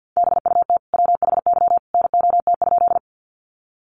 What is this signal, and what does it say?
Morse code, heard as a keyed tone, about 50 wpm, 700 Hz. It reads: BUT WHY NOT?
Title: Sentences from Top 600 Words 50wpm